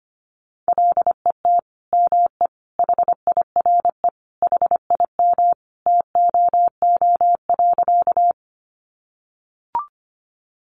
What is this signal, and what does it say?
Morse code, heard as a keyed tone, about 25 wpm, 700 Hz. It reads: LET ME HIRE HIM TOO.
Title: Sentences from Top 1400 Words 25wpm